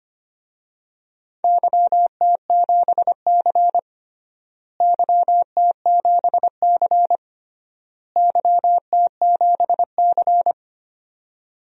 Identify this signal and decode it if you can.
Morse code, heard as a keyed tone, about 25 wpm, 700 Hz. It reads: YT7C YT7C YT7C
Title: DXCC-100 Call Signs - 4 Characters - Repeated 3x - 25wpm